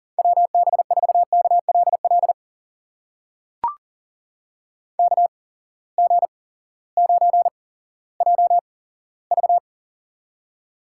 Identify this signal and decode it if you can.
Morse code, heard as a keyed tone, about 40 wpm, spaced as for 18 wpm, 700 Hz. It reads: WB4KLL KG9JV
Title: USA Call Signs - 8x Character Spacing - 40wpm